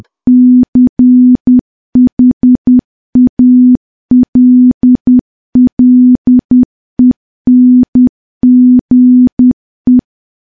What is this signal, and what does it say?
Morse code, heard as a keyed tone, about 10 wpm, 250 Hz. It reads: CHALLENGE
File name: morse / morse-training-10WPM